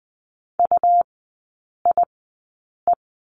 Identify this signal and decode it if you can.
Morse code, heard as a keyed tone, about 20 wpm, spaced as for 8 wpm, 700 Hz. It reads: UIE
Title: 172 Lesson 28e - Number 9 - Three Characters 20wpm